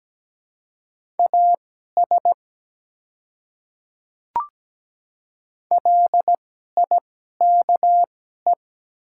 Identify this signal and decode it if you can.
Morse code, heard as a keyed tone, about 17 wpm, spaced as for 12 wpm, 690 Hz. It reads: AS LIKE
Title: 2x Character Spacing - Top 100 Words - 17wpm